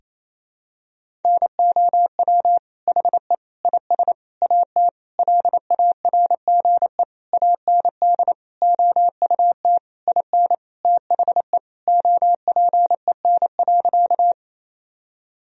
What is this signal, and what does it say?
Morse code, heard as a keyed tone, about 28 wpm, 700 Hz. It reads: NOW HE IS AT LARGE AND OUT IN THE OPEN.
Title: Sentences from Top 1000 Words - Review Part 1 - 28wpm